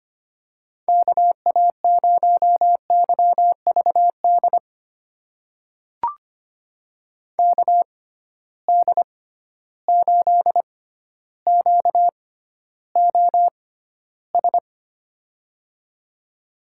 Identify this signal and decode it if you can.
Morse code, heard as a keyed tone, about 25 wpm, spaced as for 13 wpm, 700 Hz. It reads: KA0YVD KD8QOS
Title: USA Call Signs - 6 Characters - 6x Character Spacing - 25wpm